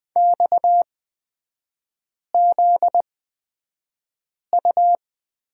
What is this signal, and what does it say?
Morse code, heard as a keyed tone, about 20 wpm, spaced as for 5 wpm, 700 Hz. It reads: XZU
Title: 255 Lesson 41e - Period - Three Characters 20wpm